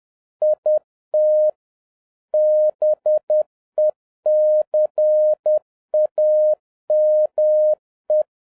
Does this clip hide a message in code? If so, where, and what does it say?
Yes: Morse code from 0.4 to 8.2 s, heard as a keyed tone, about 10 wpm, 610 Hz. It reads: IT BECAME